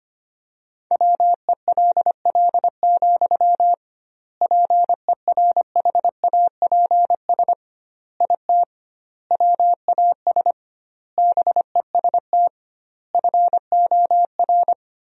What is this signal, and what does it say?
Morse code, heard as a keyed tone, about 25 wpm, 700 Hz. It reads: WELL, PERHAPS IT WAS BEST FOR
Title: 2x Word-Spacing - Sentences from Top 500 Words - 25wpm